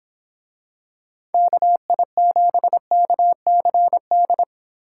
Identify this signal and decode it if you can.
Morse code, heard as a keyed tone, about 24 wpm, 700 Hz. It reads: KI7KCD